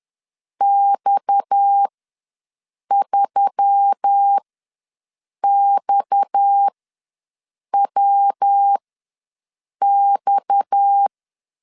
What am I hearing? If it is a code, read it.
Morse code, heard as a keyed tone, about 10 wpm, spaced as for 6 wpm, 800 Hz. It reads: X3XWX
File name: LEZIONI / 6 PKX0 / 6 PKX ws 10wpm 3706